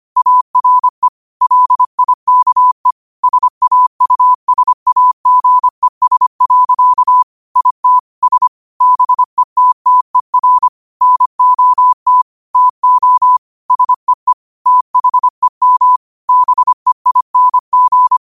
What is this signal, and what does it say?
Morse code, heard as a keyed tone, about 25 wpm, 1000 Hz. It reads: ARE LIKE SAUSAGES. IT S BETTER NOT TO SEE THEM BEING
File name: qotdfiles / 25wpm